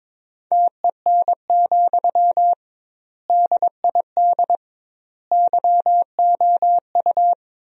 Morse code, heard as a keyed tone, about 22 wpm, 700 Hz. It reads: TEN, DID YOU
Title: 2x Word-Spacing - Sentences from Top 400 Words - 22wpm